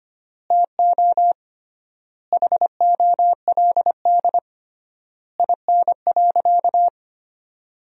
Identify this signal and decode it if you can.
Morse code, heard as a keyed tone, about 25 wpm, 700 Hz. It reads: TO HOLD IN.
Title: 3x Word-Spacing - Sentences from Top 400 Words - 25wpm